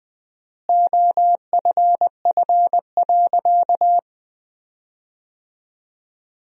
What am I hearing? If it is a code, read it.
Morse code, heard as a keyed tone, about 20 wpm, 700 Hz. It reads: OFF.
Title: 4x Word-Spacing - Sentences from Top 500 Words - 20wpm